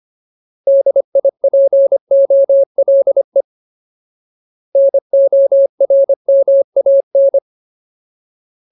Morse code, heard as a keyed tone, about 25 wpm, 550 Hz. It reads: DIPOLE NORMAN
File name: Ultimate Word List / 25WPM Ultimate